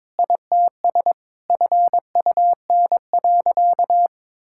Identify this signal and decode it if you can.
Morse code, heard as a keyed tone, about 22 wpm, 700 Hz. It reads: ITS FUN.
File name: practice / 22wpm / Sentences with 2-Word Phrases - 22wpm